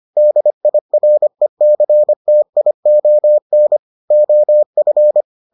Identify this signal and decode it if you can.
Morse code, heard as a keyed tone, about 25 wpm, 600 Hz. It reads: DIRECTION OF